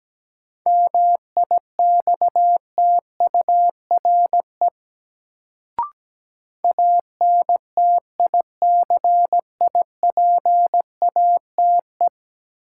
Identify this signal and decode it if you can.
Morse code, heard as a keyed tone, about 17 wpm, 700 Hz. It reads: MIXTURE ANTICIPATE
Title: Sentences from Top 1500 Words - New Word Drill Alt - 17wpm